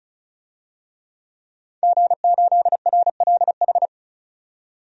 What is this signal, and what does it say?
Morse code, heard as a keyed tone, about 36 wpm, 700 Hz. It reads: G8RLH
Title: DXCC-100 Call Signs - 5 Characters - 35wpm